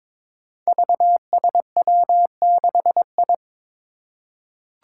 Morse code, heard as a keyed tone, about 23 wpm, 700 Hz. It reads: VSW6I
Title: Five Challenging Letter-Number Combinations 22wpm